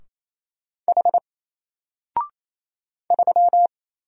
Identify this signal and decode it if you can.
Morse code, heard as a keyed tone, about 27 wpm, 700 Hz. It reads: H 3
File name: Single-Letter-Number-28wpm